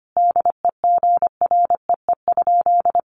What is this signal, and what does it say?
Morse code, heard as a keyed tone, about 25 wpm, 700 Hz. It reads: DEGREE?
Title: Sentences from Top 1500 Words 25wpm